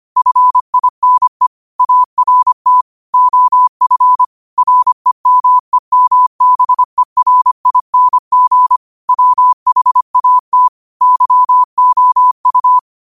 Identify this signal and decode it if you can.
Morse code, heard as a keyed tone, about 25 wpm, 1000 Hz. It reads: FINE ART OF REMEMBERING WHAT YOU
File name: qotdfiles / 25wpm